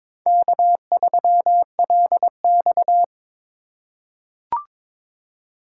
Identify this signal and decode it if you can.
Morse code, heard as a keyed tone, about 21 wpm, 700 Hz. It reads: K3LX
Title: USA Call Signs - 4 Characters - 4x Character Spacing - 22wpm